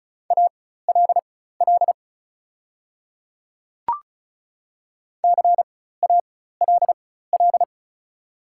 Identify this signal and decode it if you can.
Morse code, heard as a keyed tone, about 36 wpm, spaced as for 16 wpm, 700 Hz. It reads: ALL CALL
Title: 4x Character Spacing - Top 100 Words - 35wpm